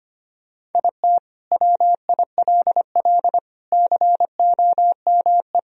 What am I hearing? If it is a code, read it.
Morse code, heard as a keyed tone, about 25 wpm, 700 Hz. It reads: IT WILL COME
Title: Sentences from Top 1000 Words - Review Part 1 - 25wpm